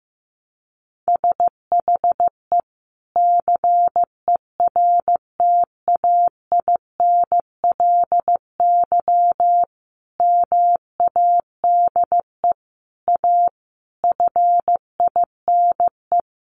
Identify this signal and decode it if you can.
Morse code, heard as a keyed tone, about 15 wpm, 700 Hz. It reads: SHE CERTAINLY MADE A FINE
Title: Sentences from Top 1100 Words 15wpm